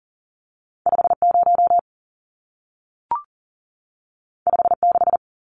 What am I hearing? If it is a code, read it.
Morse code, heard as a keyed tone, about 40 wpm, 700 Hz. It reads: 50 56
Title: Numbers - 2-Digit 40wpm